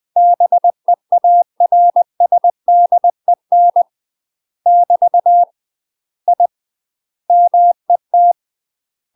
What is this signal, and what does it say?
Morse code, heard as a keyed tone, about 20 wpm, 700 Hz. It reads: BEARSDEN <BT> I MET
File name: UK MASTER QSOs TRAINING / 20WPM / UK MASTER QSOs 0053